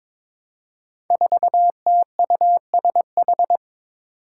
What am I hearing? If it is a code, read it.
Morse code, heard as a keyed tone, about 22 wpm, 700 Hz. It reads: 4TUSH
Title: Five Challenging Letter-Number Combinations 22wpm